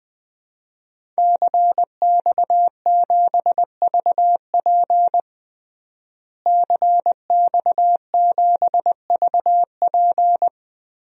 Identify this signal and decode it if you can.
Morse code, heard as a keyed tone, about 20 wpm, 700 Hz. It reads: CX7VP CX7VP
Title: DXCC-100 Call Signs - 5 Characters - Repeated 2x - 20wpm